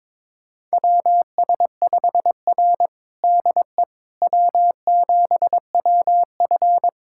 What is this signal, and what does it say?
Morse code, heard as a keyed tone, about 22 wpm, 700 Hz. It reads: WS5R DE W7WF
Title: USA Callsign DE Callsign - 4 Characters - 22wpm